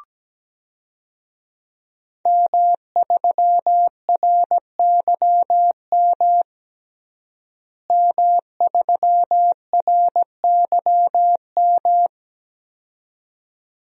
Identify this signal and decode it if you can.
Morse code, heard as a keyed tone, about 17 wpm, 700 Hz. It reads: M3RYM M3RYM
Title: DXCC-100 Call Signs - 5 Characters - Repeated 2x - 17wpm